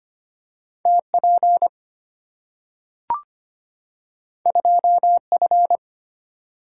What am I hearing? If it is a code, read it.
Morse code, heard as a keyed tone, about 25 wpm, 700 Hz. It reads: TP 2F
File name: Two Letter-Number Combinations - 25wpm